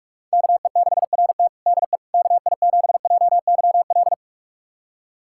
Morse code, heard as a keyed tone, about 43 wpm, 700 Hz. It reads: KE6RT DE KI7JYL
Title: USA Callsign DE Callsign 45wpm